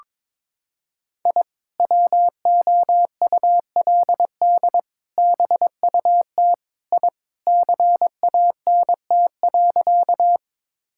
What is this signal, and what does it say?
Morse code, heard as a keyed tone, about 22 wpm, 700 Hz. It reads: I WOULD BUT I CANT.
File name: practice / 22wpm / Sentences with 2-Word Phrases - 22wpm